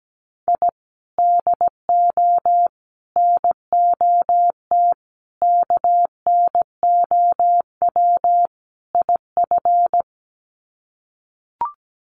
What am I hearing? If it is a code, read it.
Morse code, heard as a keyed tone, about 17 wpm, 700 Hz. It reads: I DO NOT KNOW IF
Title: Sets of 5 Words 17wpm